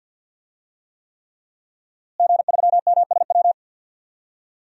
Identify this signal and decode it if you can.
Morse code, heard as a keyed tone, about 47 wpm, 700 Hz. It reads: G3MSW
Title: DXCC-100 Call Signs - 5 Characters - Repeated 3x - 50wpm